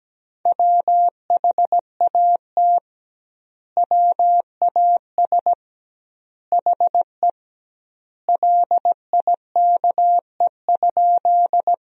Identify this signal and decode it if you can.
Morse code, heard as a keyed tone, about 17 wpm, 700 Hz. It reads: WHAT WAS HE LIKE?
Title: 2x Word-Spacing - Sentences from Top 100 Words - 17wpm